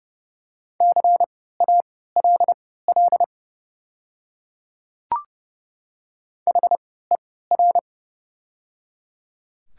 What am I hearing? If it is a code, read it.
Morse code, heard as a keyed tone, about 30 wpm, spaced as for 17 wpm, 700 Hz. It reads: CALL HER
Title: 3x Character Spacing - Top 100 Words - 30wpm